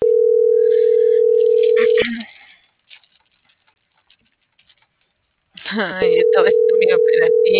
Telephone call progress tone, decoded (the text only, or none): ringback tone